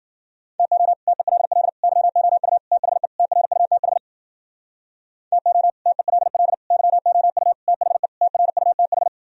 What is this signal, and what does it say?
Morse code, heard as a keyed tone, about 54 wpm, 700 Hz. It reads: TO TELL YOU THE TRUTH TO TELL YOU THE TRUTH